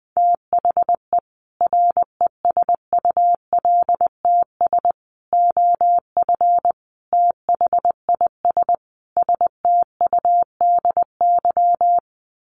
Morse code, heard as a keyed tone, about 20 wpm, 700 Hz. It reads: THE RESULTS OF THIS STUDY
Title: Sets of 5 Words 20wpm